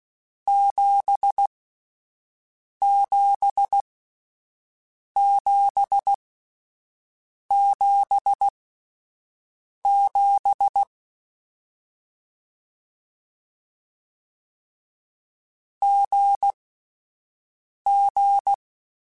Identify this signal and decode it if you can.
Morse code, heard as a keyed tone, about 16 wpm, spaced as for 5 wpm, 800 Hz. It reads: 77777 GG
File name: Difficult Combos / Difficult Combos-mp3s / GZ GZ7 @5